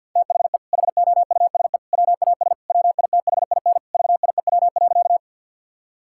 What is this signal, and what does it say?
Morse code, heard as a keyed tone, about 50 wpm, 700 Hz. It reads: THE HOUSE WAS WITHIN VIEW.
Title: Sentences from Top 1000 Words - Review Part 6 - 50wpm